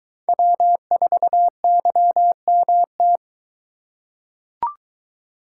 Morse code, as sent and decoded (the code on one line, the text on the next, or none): .-- ....- -.-- -- -
W4YMT